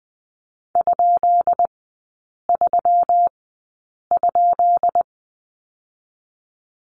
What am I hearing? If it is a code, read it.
Morse code, heard as a keyed tone, about 20 wpm, spaced as for 8 wpm, 690 Hz. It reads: ?3?
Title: 133 Lesson 22e - Question Mark - Three Characters 20wpm